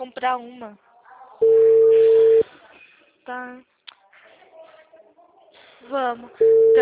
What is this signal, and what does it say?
Ringback tone (China, 450 Hz), with speech.
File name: out- -2006- - - .4996